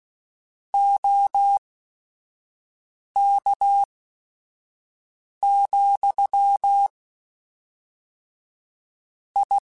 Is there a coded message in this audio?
Yes: Morse code from 0.7 to 9.6 s, heard as a keyed tone, about 16 wpm, spaced as for 5 wpm, 800 Hz. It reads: OK, I